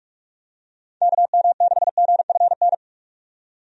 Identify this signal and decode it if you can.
Morse code, heard as a keyed tone, about 46 wpm, 700 Hz. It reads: KM6GFN
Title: USA Call Signs - 6 Characters - 45wpm